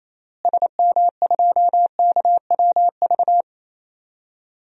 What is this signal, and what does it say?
Morse code, heard as a keyed tone, about 28 wpm, 700 Hz. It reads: SM2KWV
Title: DXCC-100 Call Signs - 6 Characters - Repeated 2x - 28wpm